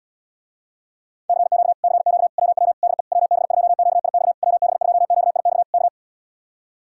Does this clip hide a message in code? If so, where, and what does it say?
Yes: Morse code from 1.3 to 5.9 s, heard as a keyed tone, about 77 wpm, 700 Hz. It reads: CQ CQ WW DE WB2QIX WB2QIX K